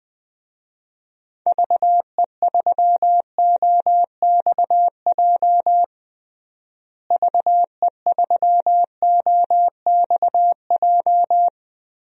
Morse code, heard as a keyed tone, about 20 wpm, 700 Hz. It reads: VE3OXJ VE3OXJ
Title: DXCC-100 Call Signs - 6 Characters - Repeated 3x - 20wpm